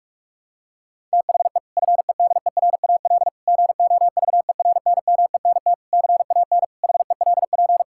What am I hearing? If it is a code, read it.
Morse code, heard as a keyed tone, about 41 wpm, 700 Hz. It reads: THE FEDERAL GOVERNMENT CAN HELP
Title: Sentences from Top 1000 Words - Review Part 5 - 45wpm